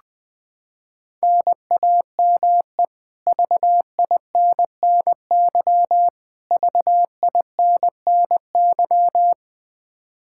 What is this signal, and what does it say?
Morse code, heard as a keyed tone, about 20 wpm, 700 Hz. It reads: NAME VINNY VINNY
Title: K1USN SST QSO Practice 20wpm